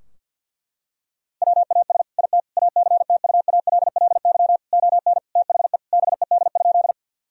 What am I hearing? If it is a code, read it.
Morse code, heard as a keyed tone, about 52 wpm, 700 Hz. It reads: WAS IT ACTUALLY ON THE BED?
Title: Sentences from Top 1000 Words - Review Part 4 - 50wpm